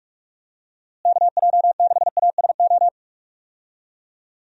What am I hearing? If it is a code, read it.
Morse code, heard as a keyed tone, about 45 wpm, 700 Hz. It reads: KJ6ASO
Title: USA Call Signs - 6 Characters - Repeated 3x - 45wpm